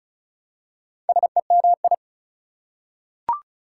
Morse code, heard as a keyed tone, about 36 wpm, 690 Hz. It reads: SEMI
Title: Prefixes and Suffixes 35wpm